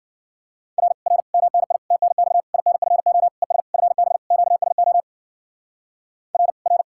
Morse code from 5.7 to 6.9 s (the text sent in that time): R R